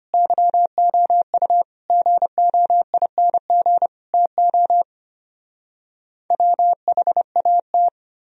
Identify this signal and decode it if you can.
Morse code, heard as a keyed tone, about 33 wpm, 700 Hz. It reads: YOU GOING TO WHAT